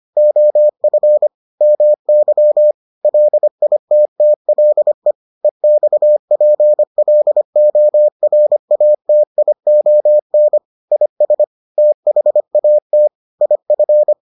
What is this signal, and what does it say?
Morse code, heard as a keyed tone, about 25 wpm, 600 Hz. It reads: OF MY LITTLE EXPLORATION IS THAT IF